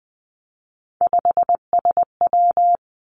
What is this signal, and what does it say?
Morse code, heard as a keyed tone, about 20 wpm, 700 Hz. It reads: 5SW
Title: Three Challenging Letter-Number Combinations 20wpm